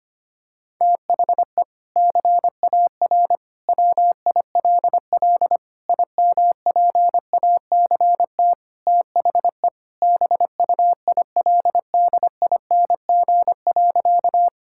Morse code, heard as a keyed tone, about 25 wpm, 700 Hz. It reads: THE CAR WILL IMPACT THE BUILDING.
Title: Sentences from Top 1000 Words - Review Part 6 - 25wpm